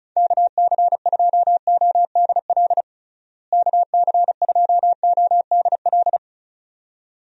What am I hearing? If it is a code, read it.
Morse code, heard as a keyed tone, about 36 wpm, 700 Hz. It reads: KC2ODL KC2ODL
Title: USA Call Signs - 6 Characters - Repeated 2x - 35wpm